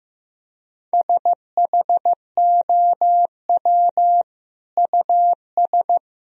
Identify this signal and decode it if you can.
Morse code, heard as a keyed tone, about 16 wpm, 700 Hz. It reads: SHOW US
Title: Sentences from Top 200 Words and Abbreviations 15wpm